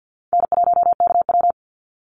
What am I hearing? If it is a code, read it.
Morse code, heard as a keyed tone, about 50 wpm, 700 Hz. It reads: N1KW
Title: Easy Call Signs - 50wpm